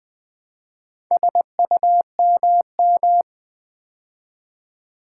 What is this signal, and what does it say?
Morse code, heard as a keyed tone, about 20 wpm, 700 Hz. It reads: SUMM
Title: Build A Word - 2 Letters At A Time - 6-Letter Words - 20wpm